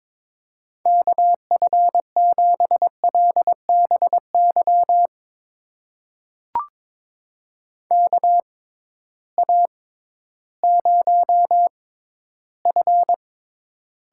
Morse code, heard as a keyed tone, about 22 wpm, spaced as for 13 wpm, 700 Hz. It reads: KF7LBY KA0F
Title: USA Call Signs - 6 Characters - 6x Character Spacing - 22wpm